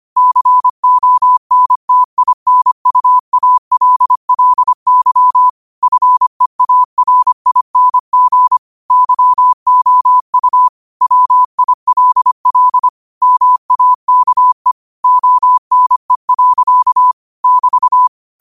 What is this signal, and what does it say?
Morse code, heard as a keyed tone, about 25 wpm, 1000 Hz. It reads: CONTINUALLY FEARING YOU WILL MAKE ONE. <BT>
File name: qotdfiles / 25wpm